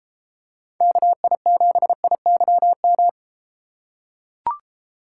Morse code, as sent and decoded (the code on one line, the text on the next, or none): -.- .. --... .. -.-- --
KI7IYM